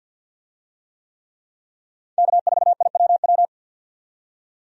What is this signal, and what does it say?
Morse code, heard as a keyed tone, about 52 wpm, 700 Hz. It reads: K4IWW
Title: USA Call Signs - 5 Characters - 50wpm